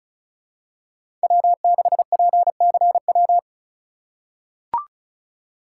Morse code, heard as a keyed tone, about 36 wpm, 700 Hz. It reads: W6PCW
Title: USA Call Signs - 5 Characters - 3x Character Spacing - 35wpm